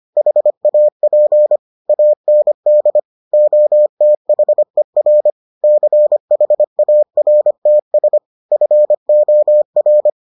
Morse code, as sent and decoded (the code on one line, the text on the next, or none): .... .- .--. / .- -. -.. / --- - .... .-. / -.-. .... .- .-. - ... / ..-. --- .-.
HAP AND OTHER CHARTS FOR